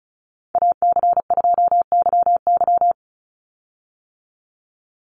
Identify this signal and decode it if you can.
Morse code, heard as a keyed tone, about 34 wpm, 700 Hz. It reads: AC2YY
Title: USA Call Signs 35wpm